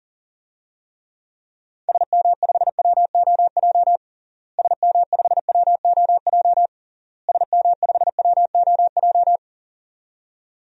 Morse code, heard as a keyed tone, about 40 wpm, 700 Hz. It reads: SM5WOJ SM5WOJ SM5WOJ
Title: DXCC-100 Call Signs - 6 Characters - Repeated 3x - 40wpm